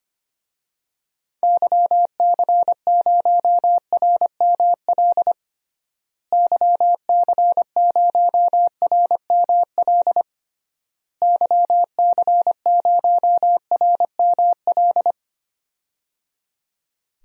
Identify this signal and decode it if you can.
Morse code, heard as a keyed tone, about 25 wpm, 700 Hz. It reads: YC0RML YC0RML YC0RML
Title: DXCC-100 Call Signs - 6 Characters - Repeated 3x - 25wpm